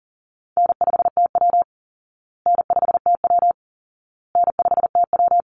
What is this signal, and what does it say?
Morse code, heard as a keyed tone, about 40 wpm, 700 Hz. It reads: N5TW N5TW N5TW
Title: Easy Call Signs Repeated 3x - 40wpm